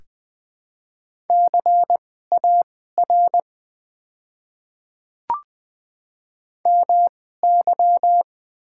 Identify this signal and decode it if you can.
Morse code, heard as a keyed tone, about 20 wpm, spaced as for 15 wpm, 700 Hz. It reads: CAR MY
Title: 2x Character Spacing - Top 300 Words - 20wpm